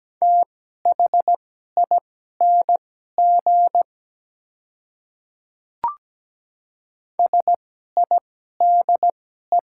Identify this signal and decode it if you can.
Morse code, heard as a keyed tone, about 17 wpm, spaced as for 12 wpm, 700 Hz. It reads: THING SIDE
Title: 2x Character Spacing - Top 100 Words - 17wpm